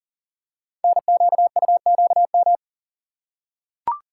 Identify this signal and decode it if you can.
Morse code, heard as a keyed tone, about 39 wpm, 690 Hz. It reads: NQUQM
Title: Code Groups - 5 Letters - 40wpm